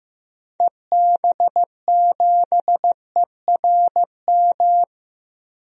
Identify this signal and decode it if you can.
Morse code, heard as a keyed tone, about 15 wpm, 700 Hz. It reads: EB7ERM
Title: DXCC-100 Call Signs - 6 Characters - Repeated 2x - 15wpm